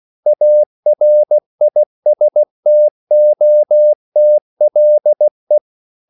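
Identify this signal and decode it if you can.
Morse code, heard as a keyed tone, about 16 wpm, 600 Hz. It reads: ARISTOTLE
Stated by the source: English quotations sent in CW as practice at 16 words per minute.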